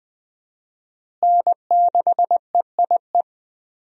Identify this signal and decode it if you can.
Morse code, heard as a keyed tone, about 20 wpm, 700 Hz. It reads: N6EIE
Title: USA Call Signs - 5 Characters - Repeated 3x - 20wpm